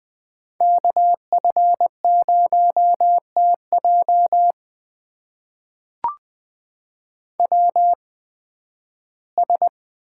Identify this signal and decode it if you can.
Morse code, heard as a keyed tone, about 20 wpm, spaced as for 13 wpm, 700 Hz. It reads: KF0TJ WS
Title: USA Call Signs - 8x Character Spacing - 20wpm